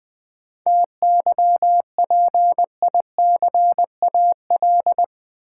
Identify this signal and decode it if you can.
Morse code, heard as a keyed tone, about 20 wpm, 700 Hz. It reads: TYPICAL
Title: Sentences from Top 1100 Words - New Word Drill - 20wpm